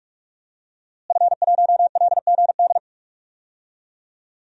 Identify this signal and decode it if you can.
Morse code, heard as a keyed tone, about 44 wpm, 700 Hz. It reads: F1LGD
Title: DXCC-100 Call Signs - 5 Characters - Repeated 3x - 45wpm